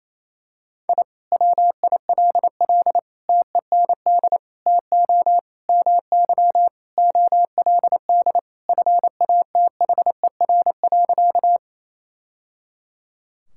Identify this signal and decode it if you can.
Morse code, heard as a keyed tone, about 28 wpm, 700 Hz. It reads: I WILL TEND TO MY OLD FATHER.
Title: Sentences from Top 1000 Words - Review Part 5 - 28wpm